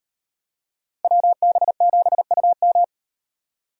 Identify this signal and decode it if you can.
Morse code, heard as a keyed tone, about 38 wpm, 700 Hz. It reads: WB7UM